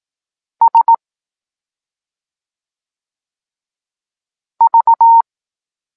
Morse code, heard as a keyed tone, about 18 wpm, 920 Hz. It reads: S V
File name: Tekst-9-rownomierny